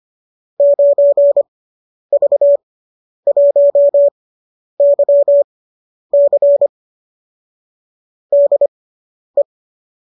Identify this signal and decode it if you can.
Morse code, heard as a keyed tone, about 25 wpm, spaced as for 10 wpm, 570 Hz. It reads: 9V1YC DE